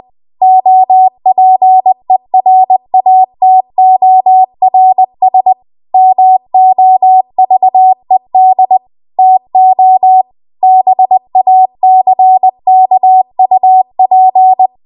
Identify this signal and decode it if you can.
Morse code, heard as a keyed tone, about 20 wpm, 750 Hz. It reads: OPERATORS MOVED TO BACKUP